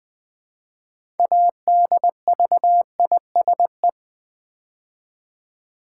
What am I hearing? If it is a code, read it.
Morse code, heard as a keyed tone, about 20 wpm, 700 Hz. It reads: ADVISE
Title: Build A Word - 3 Letters At A Time - 6-Letter Words - 20wpm